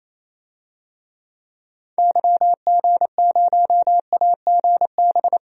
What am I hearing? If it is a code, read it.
Morse code, heard as a keyed tone, about 28 wpm, 700 Hz. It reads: YG0AGB
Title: DXCC-100 Call Signs - 6 Characters - 28wpm